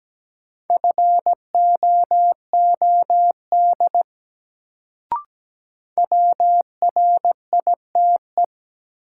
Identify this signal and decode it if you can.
Morse code, heard as a keyed tone, about 17 wpm, 700 Hz. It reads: FOOD WRITE
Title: Sentences from Top 300 Words - New Word Drill - 17wpm